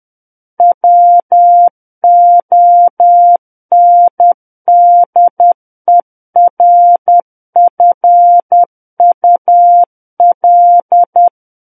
Morse code, heard as a keyed tone, about 10 wpm, 700 Hz. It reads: WONDERFUL
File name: Book One / 01-Chapter 1 - 10 WPM